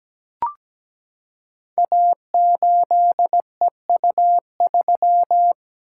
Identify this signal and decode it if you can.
Morse code, heard as a keyed tone, about 17 wpm, 700 Hz. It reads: A8EU3
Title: Code Groups - 5 Letter:Numbers - 17wpm